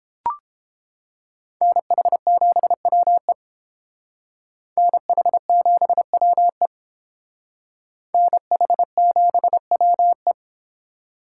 Morse code, heard as a keyed tone, about 30 wpm, 700 Hz. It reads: NH7WE NH7WE NH7WE